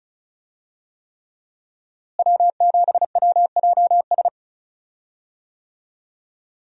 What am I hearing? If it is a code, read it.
Morse code, heard as a keyed tone, about 36 wpm, 700 Hz. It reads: W7WJS
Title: USA Call Signs - 5 Characters - Repeated 3x - 35wpm